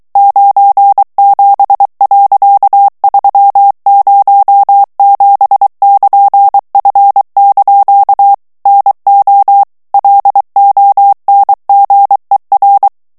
Morse code, heard as a keyed tone, about 23 wpm, 800 Hz. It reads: 97.307<KN>F) NO LONGER
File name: ANGLAIS texte clair 3 vitesse 22 mots minute